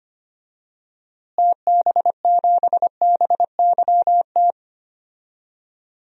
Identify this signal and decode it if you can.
Morse code, heard as a keyed tone, about 25 wpm, 700 Hz. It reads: TB7BYT
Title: DXCC-100 Call Signs - 6 Characters - Repeated 3x - 25wpm